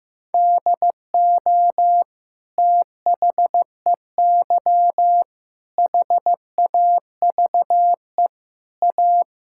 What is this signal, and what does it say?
Morse code, heard as a keyed tone, about 15 wpm, 700 Hz. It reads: DO THEY HAVE A
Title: Sentences from Top 1000 Words - Review Part 6 - 15wpm